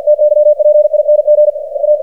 Morse code, heard as a keyed tone, about 51 wpm, 600 Hz, with radio noise. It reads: TYPING A
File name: archived / CRSnet